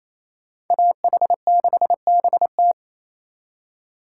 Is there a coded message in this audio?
Yes: Morse code from 0.7 to 2.7 s, heard as a keyed tone, about 36 wpm, 700 Hz. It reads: AH6BT